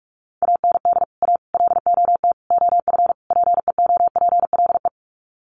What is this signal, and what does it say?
Morse code, heard as a keyed tone, about 44 wpm, 700 Hz. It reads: AND A LOT OF PEOPLE
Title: Sets of 5 Words 45wpm